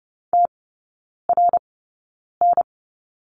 Morse code, heard as a keyed tone, about 30 wpm, spaced as for 9 wpm, 690 Hz. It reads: TRN
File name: Lesson 17e - Letter u - Three Characters 30wpm